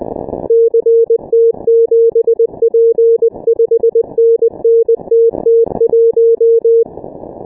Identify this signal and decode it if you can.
Morse code, heard as a keyed tone, about 20 wpm, 460 Hz. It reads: CT7P 5NN TT1
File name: dk5ke / Contest2QSOs 100BpM